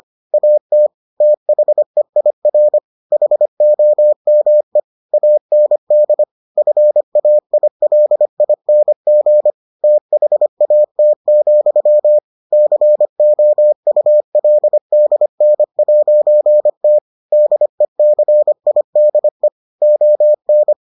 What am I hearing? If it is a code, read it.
Morse code, heard as a keyed tone, about 25 wpm, 600 Hz. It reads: AT THEIR HOME AND FAILING THAT, COULDN'T DECIDE ON